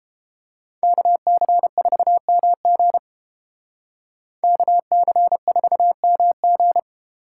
Morse code, heard as a keyed tone, about 31 wpm, 700 Hz. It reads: KC4MG KC4MG